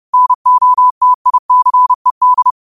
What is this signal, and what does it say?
Morse code, heard as a keyed tone, about 30 wpm, 1000 Hz. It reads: NOTICED
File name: qotdfiles / 30wpm